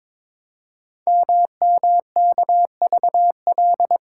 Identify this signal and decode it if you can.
Morse code, heard as a keyed tone, about 22 wpm, 700 Hz. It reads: MMKVL
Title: Code Groups - 5 Letters - 22wpm